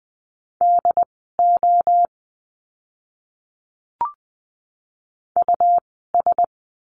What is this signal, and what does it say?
Morse code, heard as a keyed tone, about 20 wpm, spaced as for 14 wpm, 700 Hz. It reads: DO US
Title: 115 Lesson 19f - Letter w - Words 20wpm